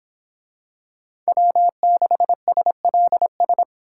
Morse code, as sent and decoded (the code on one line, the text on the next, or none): .-- -.... ... .-.. ...
W6SLS